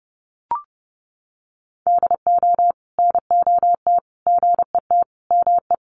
Morse code, heard as a keyed tone, about 30 wpm, 700 Hz. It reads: DO NOT GET ME
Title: Sets of 5 Words Encore 30wpm